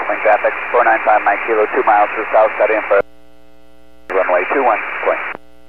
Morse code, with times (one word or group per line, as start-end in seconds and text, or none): none